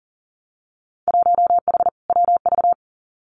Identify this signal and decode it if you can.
Morse code, heard as a keyed tone, about 38 wpm, 700 Hz. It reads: 1H WV
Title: Winter Field Day Exchanges Encore 40wpm